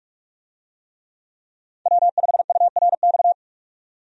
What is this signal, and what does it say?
Morse code, heard as a keyed tone, about 44 wpm, 700 Hz. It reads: W5URX
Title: USA Call Signs - 5 Characters - 45wpm